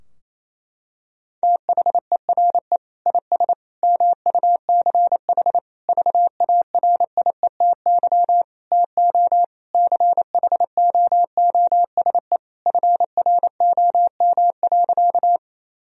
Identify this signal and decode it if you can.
Morse code, heard as a keyed tone, about 28 wpm, 700 Hz. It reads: THERE IS MUCH VARIETY TO CHOOSE FROM.
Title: Sentences from Top 1000 Words - Review Part 4 - 28wpm